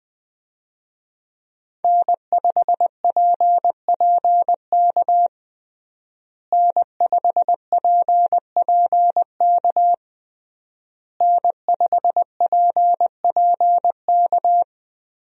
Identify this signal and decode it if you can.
Morse code, heard as a keyed tone, about 20 wpm, 700 Hz. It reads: N5PPK N5PPK N5PPK
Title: USA Call Signs - 5 Characters - Repeated 3x - 20wpm